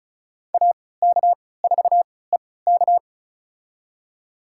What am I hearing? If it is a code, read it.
Morse code, heard as a keyed tone, about 36 wpm, spaced as for 20 wpm, 700 Hz. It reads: AK4EK
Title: USA Call Signs - 3x Character Spacing - 35wpm